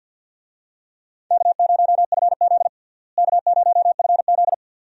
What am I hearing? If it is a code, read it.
Morse code, heard as a keyed tone, about 53 wpm, 700 Hz. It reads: K0FZ K0FZ
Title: USA Call Signs - 4 Characters - Repeated 2x - 50wpm